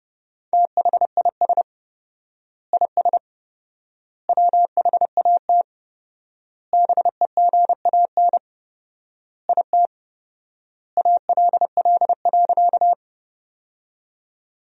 Morse code, heard as a keyed tone, about 32 wpm, 700 Hz. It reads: THIS IS WHAT BEGAN IT ALL.
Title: 4x Word-Spacing - Sentences from Top 300 Words - 30wpm